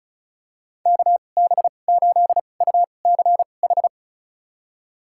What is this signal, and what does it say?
Morse code, heard as a keyed tone, about 36 wpm, spaced as for 25 wpm, 700 Hz. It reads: KB8UCH
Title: USA Call Signs - 6 Characters - 2x Character Spacing - 35wpm